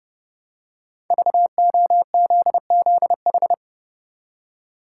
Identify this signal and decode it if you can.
Morse code, heard as a keyed tone, about 30 wpm, 700 Hz. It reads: VOZZH
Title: Code Groups - 5 Letter:Numbers - 30wpm